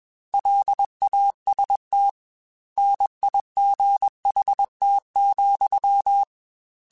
Morse code, heard as a keyed tone, about 21 wpm, 800 Hz. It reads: LAST NIGHT,